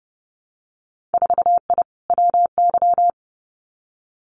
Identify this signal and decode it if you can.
Morse code, heard as a keyed tone, about 30 wpm, 700 Hz. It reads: 4I WY
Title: Winter Field Day Exchanges Encore 30wpm